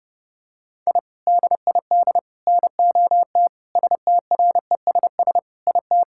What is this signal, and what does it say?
Morse code, heard as a keyed tone, about 28 wpm, 700 Hz. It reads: I DID NOT STRESS IT
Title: Sentences from Top 1000 Words - Review Part 6 - 30wpm